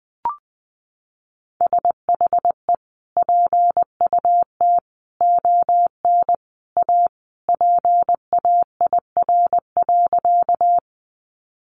Morse code, heard as a keyed tone, about 20 wpm, 700 Hz. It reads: SHE PUT ON A PAIR.
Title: Sentences from Top 1200 Words 20wpm